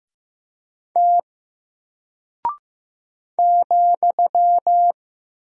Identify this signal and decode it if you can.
Morse code, heard as a keyed tone, about 15 wpm, 700 Hz. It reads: T ,
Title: Single Character (Letter:Number:Punctuation:Prosign) Alt 15wpm